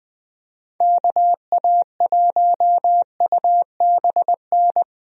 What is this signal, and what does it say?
Morse code, heard as a keyed tone, about 20 wpm, 700 Hz. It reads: KA1UBN
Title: USA Call Signs - 6 Characters - Repeated 2x - 20wpm